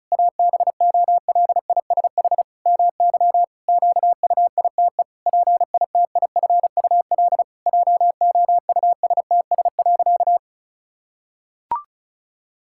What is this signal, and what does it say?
Morse code, heard as a keyed tone, about 36 wpm, 700 Hz. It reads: ABOLISH MY QUITE PITIFUL JOUSTS.
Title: Pangram Challenge v2 - 35wpm